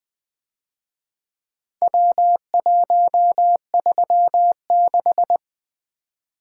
Morse code, heard as a keyed tone, about 23 wpm, 700 Hz. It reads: W136
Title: Speed-Racing - Four Letter-Number Combinations - 20wpm